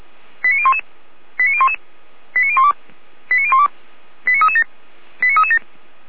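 Five-tone selective call (ZVEI), heard as quick runs of five tones, tones of about 70 ms. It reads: 89011 89011 89012 89012 89337 89337
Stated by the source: ZVEI five-tone call received on an SDR.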